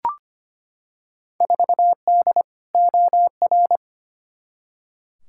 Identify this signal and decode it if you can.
Morse code, heard as a keyed tone, about 25 wpm, 700 Hz. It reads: 4D OR
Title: Field Day Exchanges Encore - 25wpm